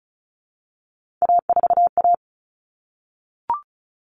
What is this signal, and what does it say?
Morse code, heard as a keyed tone, about 34 wpm, 700 Hz. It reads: A4A
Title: All Cut Numbers - 3-Digit 35wpm